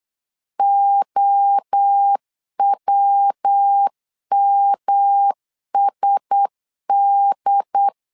Morse code, heard as a keyed tone, about 8 wpm, 800 Hz. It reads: OWMSD
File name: LEZIONI / 4 Q1S 8wpm 2710